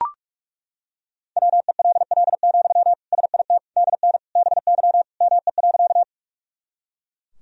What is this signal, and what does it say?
Morse code, heard as a keyed tone, about 42 wpm, 700 Hz. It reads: WELL, SIT DN BY ME.
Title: Sentences from Top 500 Words and Abbreviations 45wpm